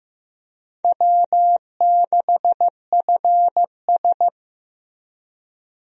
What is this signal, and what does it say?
Morse code, heard as a keyed tone, about 15 wpm, 700 Hz. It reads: W6FS
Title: USA Call Signs - 4 Characters - Repeated 2x - 15wpm